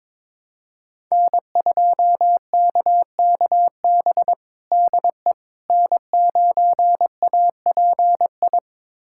Morse code, heard as a keyed tone, about 22 wpm, 700 Hz. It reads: N2KKB DE N9API
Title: USA Callsign DE Callsign - 5 Characters - 22wpm